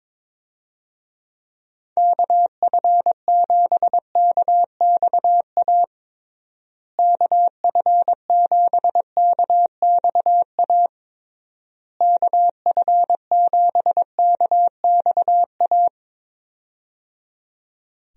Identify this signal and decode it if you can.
Morse code, heard as a keyed tone, about 22 wpm, 700 Hz. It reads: KF7KXA KF7KXA KF7KXA
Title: USA Call Signs - 6 Characters - Repeated 3x - 22wpm